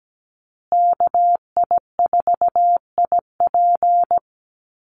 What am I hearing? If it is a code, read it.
Morse code, heard as a keyed tone, about 17 wpm, 700 Hz. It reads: KI4IP
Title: USA Call Signs Repeated 3x 17wpm